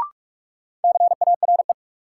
Morse code, heard as a keyed tone, about 46 wpm, 700 Hz. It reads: CARE
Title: Sets of 3 Words - New Word Drill - 45wpm